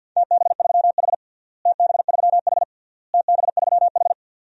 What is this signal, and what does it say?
Morse code, heard as a keyed tone, about 47 wpm, 700 Hz. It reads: TB3H TB3H TB3H
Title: DXCC-100 Call Signs - 4 Characters - Repeated 3x - 50wpm